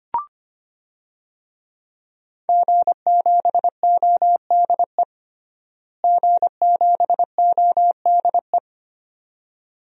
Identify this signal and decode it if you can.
Morse code, heard as a keyed tone, about 25 wpm, 700 Hz. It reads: G7ODE G7ODE
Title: DXCC-100 Call Signs - 5 Characters - Repeated 2x - 25wpm